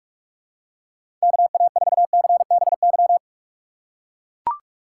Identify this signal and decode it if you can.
Morse code, heard as a keyed tone, about 42 wpm, 700 Hz. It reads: KA4CBY
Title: USA Call Signs - 6x Character Spacing - 45wpm